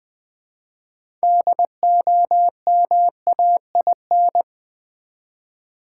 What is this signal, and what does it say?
Morse code, heard as a keyed tone, about 20 wpm, 700 Hz. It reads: DOMAIN
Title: 6 Letter Words - First 3 Letters Given - 20wpm